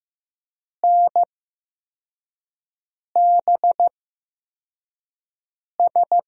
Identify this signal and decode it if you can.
Morse code, heard as a keyed tone, about 15 wpm, spaced as for 4 wpm, 700 Hz. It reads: NBS
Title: USA Call Signs - 5 Characters - 8x Character Spacing - 15wpm